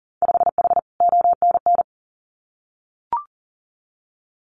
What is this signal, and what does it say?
Morse code, heard as a keyed tone, about 38 wpm, 700 Hz. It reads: 5H ONN
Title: Winter Field Day Exchanges Encore 40wpm